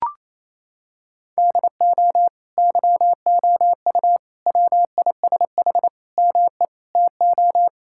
Morse code, heard as a keyed tone, about 28 wpm, 700 Hz. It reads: DO YOU WISH ME TO
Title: Sentences from Top 1600 Words 28wpm